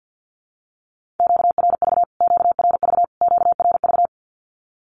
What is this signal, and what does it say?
Morse code, heard as a keyed tone, about 50 wpm, 700 Hz. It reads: QRV QRV QRV
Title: Q-Codes Repeated 3x 50wpm